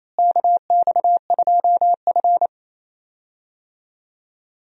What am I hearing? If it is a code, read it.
Morse code, heard as a keyed tone, about 27 wpm, 700 Hz. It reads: KX2F
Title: USA Call Signs - 4 Characters - Repeated 2x - 28wpm